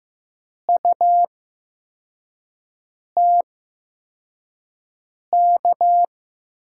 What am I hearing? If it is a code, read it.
Morse code, heard as a keyed tone, about 15 wpm, spaced as for 4 wpm, 700 Hz. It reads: UTK
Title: USA Call Signs - 5 Characters - 8x Character Spacing - 15wpm